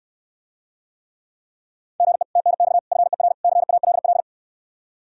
Morse code, heard as a keyed tone, about 69 wpm, 700 Hz. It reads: GE TNX FER CALL